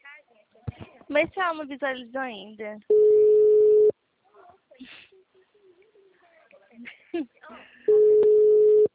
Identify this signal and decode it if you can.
Ringback tone (Europe, 425 Hz), with speech.
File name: out- -2019- - - .12006